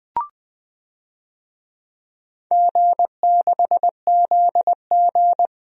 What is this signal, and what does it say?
Morse code, heard as a keyed tone, about 21 wpm, 700 Hz. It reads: G6ZG
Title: DXCC-100 Call Signs - 4 Characters - Repeated 3x - 20wpm